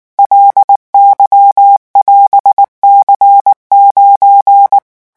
Morse code, heard as a keyed tone, about 19 wpm, 800 Hz. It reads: LY<AS>C9
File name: vitesse 18 mots